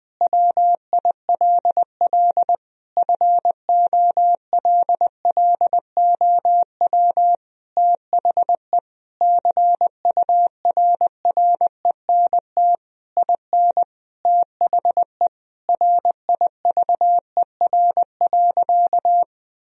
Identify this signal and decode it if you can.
Morse code, heard as a keyed tone, about 20 wpm, 700 Hz. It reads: WILL FOLLOW THE CURRENT IN THE RIVER.
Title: Sentences from Top 1000 Words - Review Part 4 - 20wpm